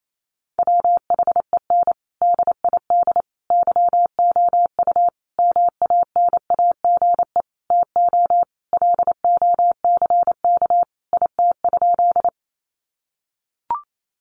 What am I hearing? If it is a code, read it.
Morse code, heard as a keyed tone, about 28 wpm, 700 Hz. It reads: WHEN DID YOU MANAGE TO LOCK IT?
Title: Sentences from Top 1400 Words 28wpm